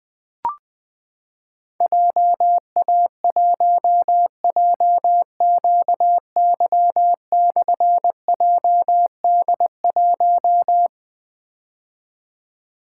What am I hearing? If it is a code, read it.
Morse code, heard as a keyed tone, about 20 wpm, 700 Hz. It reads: JA1JQY/JD1
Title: Difficult Call Signs 20wpm